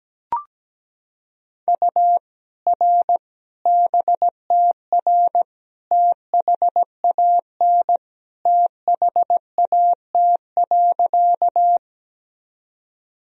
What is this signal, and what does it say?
Morse code, heard as a keyed tone, about 17 wpm, 700 Hz. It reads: U R BTR THAN THAT.
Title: Sentences from Top 400 Words and Abbreviations 17wpm